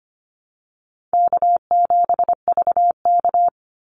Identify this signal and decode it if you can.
Morse code, heard as a keyed tone, about 25 wpm, 700 Hz. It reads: K7VK
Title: NA SOTA Activator Callsigns 25wpm